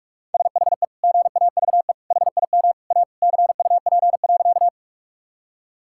Morse code, heard as a keyed tone, about 41 wpm, 700 Hz. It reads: SHE GAVE HIM A CUP.
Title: Sentences from Top 1600 Words 45wpm